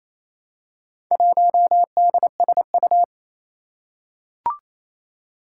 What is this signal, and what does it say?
Morse code, heard as a keyed tone, about 27 wpm, 700 Hz. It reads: 1DSU
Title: Four Challenging Letter-Number Combinations 28wpm